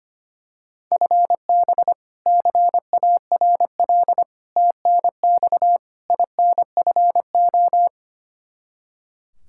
Morse code, heard as a keyed tone, about 25 wpm, 700 Hz. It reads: FB CARL TNX INFO
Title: Practice QSO - Exchange 3 - More Info - 25wpm